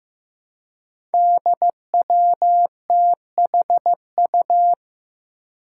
Morse code, heard as a keyed tone, about 15 wpm, 700 Hz. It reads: DWTHU
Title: Five Challenging Letter-Number Combinations 15wpm